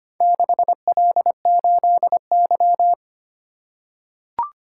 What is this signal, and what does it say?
Morse code, heard as a keyed tone, about 25 wpm, 700 Hz. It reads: 6L8Y
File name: practice / 25wpm / Four Letter-Number Combinations - 25wpm